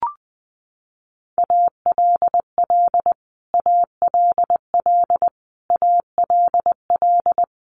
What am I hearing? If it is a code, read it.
Morse code, heard as a keyed tone, about 20 wpm, 700 Hz. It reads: ALL ALL ALL
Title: Top 200 Words Repeated 20wpm